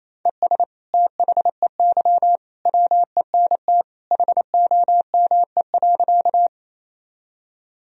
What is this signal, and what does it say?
Morse code, heard as a keyed tone, about 28 wpm, 700 Hz. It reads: ES THEY WENT HOME.
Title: Sentences from Top 200 Words and Abbreviations 28wpm